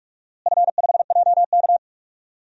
Morse code, heard as a keyed tone, about 46 wpm, 700 Hz. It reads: F5JK
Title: DXCC-100 Call Signs - 4 Characters - Repeated 3x - 45wpm